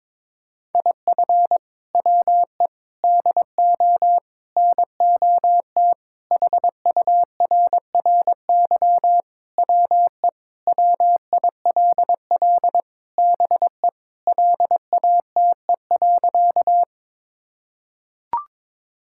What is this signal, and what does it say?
Morse code, heard as a keyed tone, about 22 wpm, 700 Hz. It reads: IF WE DO NOT HURRY WE WILL BE LATE.
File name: practice / 22wpm / Sentences with 4-Word Phrases - 22wpm